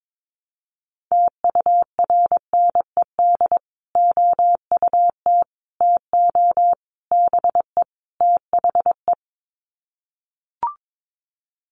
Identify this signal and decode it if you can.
Morse code, heard as a keyed tone, about 22 wpm, 700 Hz. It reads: TURNED OUT TO BE THE
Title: Sets of 5 Words Encore 22wpm